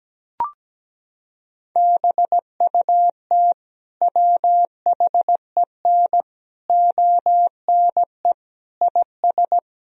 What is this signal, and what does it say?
Morse code, heard as a keyed tone, about 17 wpm, 700 Hz. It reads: BUT WHEN ONE IS
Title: Sentences from Top 1000 Words - Review Part 1 - 17wpm